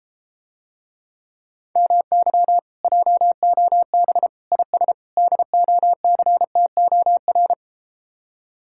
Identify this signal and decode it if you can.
Morse code, heard as a keyed tone, about 33 wpm, 700 Hz. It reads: MY JOB IS DOCTOR